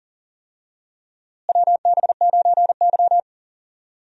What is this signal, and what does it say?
Morse code, heard as a keyed tone, about 39 wpm, 700 Hz. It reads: WB9Y
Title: USA Call Signs - 4 Characters - Repeated 3x - 40wpm